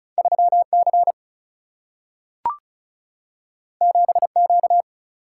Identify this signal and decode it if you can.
Morse code, heard as a keyed tone, about 36 wpm, 700 Hz. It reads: 3C 7Q
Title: Single-Letter Single-Number 35wpm